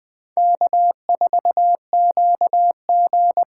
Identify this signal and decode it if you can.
Morse code, heard as a keyed tone, about 24 wpm, 700 Hz. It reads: K4QG
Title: Speed Racing - USA 1x2 Call Signs 20wpm